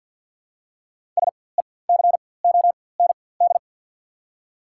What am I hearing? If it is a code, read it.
Morse code, heard as a keyed tone, about 51 wpm, spaced as for 23 wpm, 700 Hz. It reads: SECOND
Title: 4x Character Spacing - Top 300 Words - 50wpm